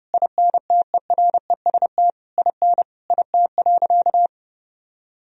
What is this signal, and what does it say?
Morse code, heard as a keyed tone, about 32 wpm, 700 Hz. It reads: INTEREST IN IT.
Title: Sentences from Top 1000 Words - Review Part 2 - 30wpm